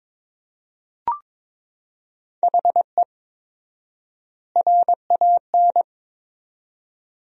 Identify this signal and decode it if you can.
Morse code, heard as a keyed tone, about 22 wpm, 700 Hz. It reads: HE RAN
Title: 4x Word-Spacing - Sentences from Top 500 Words - 22wpm